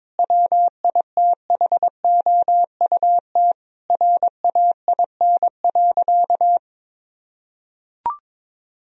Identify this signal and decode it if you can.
Morse code, heard as a keyed tone, about 22 wpm, 700 Hz. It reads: WITHOUT RAIN.
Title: Sentences from Top 1000 Words - Review Part 3 - 22wpm